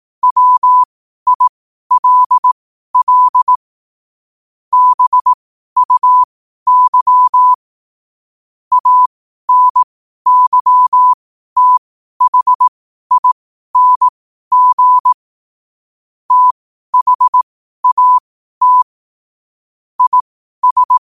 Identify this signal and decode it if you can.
Morse code, heard as a keyed tone, about 18 wpm, spaced as for 12 wpm, 1000 Hz. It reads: WILL BUY ANYTHING THAT IS
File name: qotdfiles / 12wpm